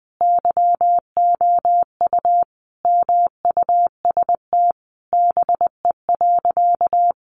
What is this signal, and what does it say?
Morse code, heard as a keyed tone, about 20 wpm, 700 Hz. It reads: YOU MUST BE.
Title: Sentences from Top 1400 Words 20wpm